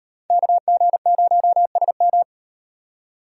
Morse code, heard as a keyed tone, about 37 wpm, 700 Hz. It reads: KG0SM